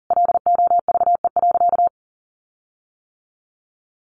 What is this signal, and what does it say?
Morse code, heard as a keyed tone, about 40 wpm, 700 Hz. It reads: LOVE.
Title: Sentences from Top 1500 Words 40wpm